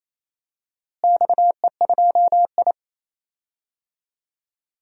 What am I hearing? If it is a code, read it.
Morse code, heard as a keyed tone, about 28 wpm, 700 Hz. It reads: XE2I
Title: DXCC-100 Call Signs - 4 Characters - Repeated 3x - 28wpm